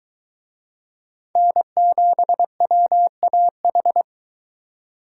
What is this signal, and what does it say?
Morse code, heard as a keyed tone, about 22 wpm, 700 Hz. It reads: N7WAH